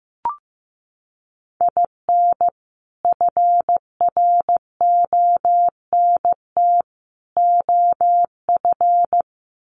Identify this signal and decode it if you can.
Morse code, heard as a keyed tone, about 15 wpm, 700 Hz. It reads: IN FRONT OF
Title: Sets of 3 Words 15wpm